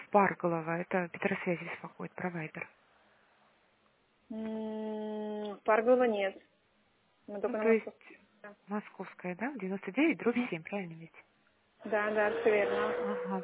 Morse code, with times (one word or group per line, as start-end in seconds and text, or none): none